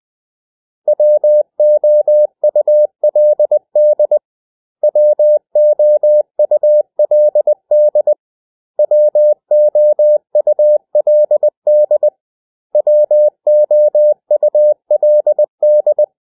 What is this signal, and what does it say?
Morse code, heard as a keyed tone, about 20 wpm, 600 Hz. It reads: WOULD WOULD WOULD WOULD
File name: cw / 100 Common English Words